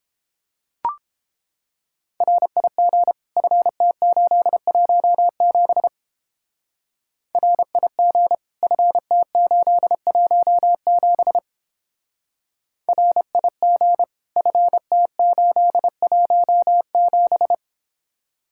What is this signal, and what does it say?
Morse code, heard as a keyed tone, about 31 wpm, 700 Hz. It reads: RIG FT817 RIG FT817 RIG FT817